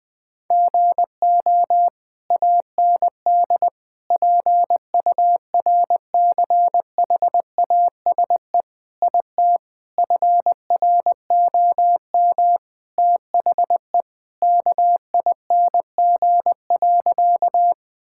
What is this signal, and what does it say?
Morse code, heard as a keyed tone, about 20 wpm, 700 Hz. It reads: GO AND PURCHASE IT FROM THE KING.
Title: Sentences from Top 1000 Words - Review Part 5 - 20wpm